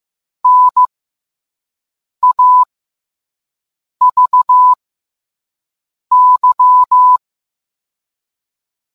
Morse code, heard as a keyed tone, about 15 wpm, spaced as for 5 wpm, 1000 Hz. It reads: NAVY